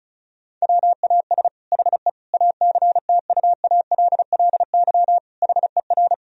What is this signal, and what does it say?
Morse code, heard as a keyed tone, about 34 wpm, 700 Hz. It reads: WAS HE ACTUALLY HER
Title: Sentences from Top 1000 Words - Review Part 4 - 35wpm